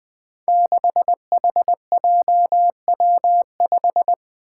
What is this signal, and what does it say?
Morse code, heard as a keyed tone, about 20 wpm, 700 Hz. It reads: 6HJW5
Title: Five Challenging Letter-Number Combinations 20wpm